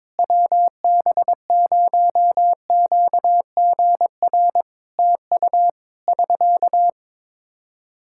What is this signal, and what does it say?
Morse code, heard as a keyed tone, about 22 wpm, 700 Hz. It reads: WB0QGR TU <SK>
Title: Practice QSO - Exchange 4 - Ending - 22wpm